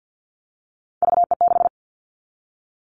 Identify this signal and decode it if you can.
Morse code, heard as a keyed tone, about 50 wpm, 700 Hz. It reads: VE6
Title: Callsign Prefixes for Top 100 DXCC - 50wpm